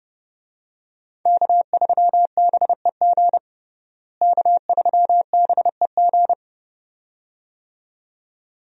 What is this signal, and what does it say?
Morse code, heard as a keyed tone, about 30 wpm, 700 Hz. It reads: K3BEG K3BEG
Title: USA Call Signs - 5 Characters - Repeated 2x - 30wpm